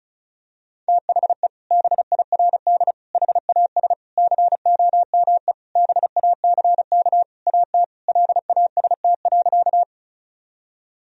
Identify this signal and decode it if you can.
Morse code, heard as a keyed tone, about 34 wpm, 700 Hz. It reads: THE BIRD HAS COME BACK AT LAST.
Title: Sentences from Top 1000 Words - Review Part 2 - 35wpm